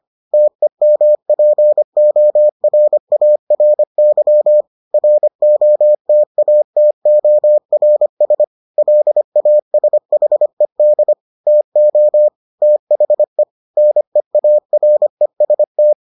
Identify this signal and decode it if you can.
Morse code, heard as a keyed tone, about 25 wpm, 600 Hz. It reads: TEMPORARY ROTATORS LASHED TO THE NEAREST